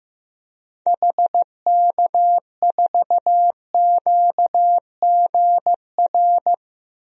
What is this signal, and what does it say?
Morse code, heard as a keyed tone, about 15 wpm, 700 Hz. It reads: HK4QGR
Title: DXCC-100 Call Signs - 6 Characters - 15wpm